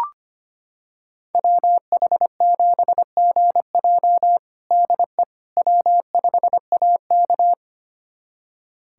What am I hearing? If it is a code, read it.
Morse code, heard as a keyed tone, about 25 wpm, 700 Hz. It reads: WH7GJ DE W5AK
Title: USA Callsign DE Callsign 25wpm